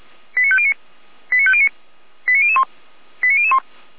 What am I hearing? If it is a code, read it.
Five-tone selective call (ZVEI), heard as quick runs of five tones, tones of about 70 ms. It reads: 89509 89509 89001 89001